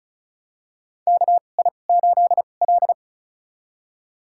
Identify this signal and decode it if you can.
Morse code, heard as a keyed tone, about 36 wpm, spaced as for 25 wpm, 700 Hz. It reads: KI8L
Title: USA Call Signs - 2x Character Spacing - 35wpm